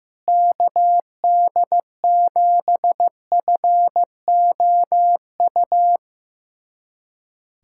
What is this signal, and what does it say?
Morse code, heard as a keyed tone, about 15 wpm, 700 Hz. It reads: KD7FOU
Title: USA Callsign DE Callsign 15wpm